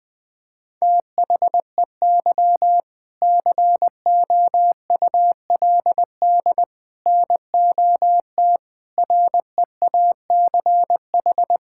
Morse code, heard as a keyed tone, about 20 wpm, 700 Hz. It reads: THEY COULD NOT REACH
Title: Sentences from Top 1000 Words - Review Part 6 - 20wpm